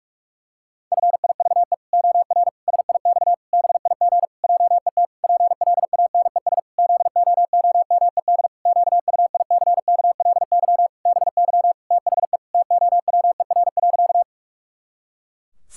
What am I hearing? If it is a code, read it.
Morse code, heard as a keyed tone, about 46 wpm, 700 Hz. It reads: FIVE OR SIX BIG JET PLANES ZOOMED QUICKLY BY THE TOWER.
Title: Pangram Challenge v2 - 45wpm